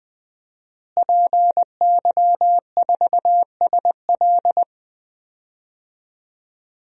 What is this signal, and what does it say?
Morse code, heard as a keyed tone, about 20 wpm, 700 Hz. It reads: PY4SL
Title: DXCC-100 Call Signs - 5 Characters - Repeated 3x - 20wpm